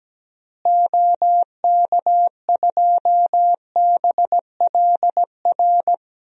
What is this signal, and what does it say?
Morse code, heard as a keyed tone, about 17 wpm, 700 Hz. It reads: OK2BLR
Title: DXCC-100 Call Signs - 6 Characters - Repeated 3x - 17wpm